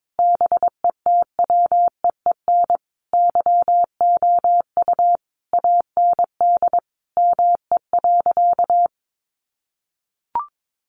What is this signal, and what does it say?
Morse code, heard as a keyed tone, about 22 wpm, 700 Hz. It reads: BETWEEN YOU AND ME.
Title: Sentences from Top 1200 Words 22wpm